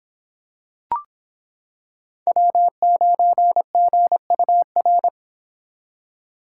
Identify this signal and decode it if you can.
Morse code, heard as a keyed tone, about 26 wpm, 700 Hz. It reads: W9GUR